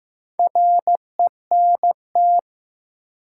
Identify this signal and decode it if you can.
Morse code, heard as a keyed tone, about 15 wpm, 700 Hz. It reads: RENT
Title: Sentences from Top 1000 Words - Review Part 5 - New Word Drill - 15wpm